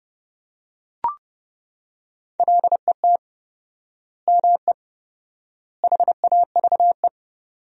Morse code, heard as a keyed tone, about 30 wpm, 700 Hz. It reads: LET ME HAVE
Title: 4x Word-Spacing - Sentences from Top 300 Words - 30wpm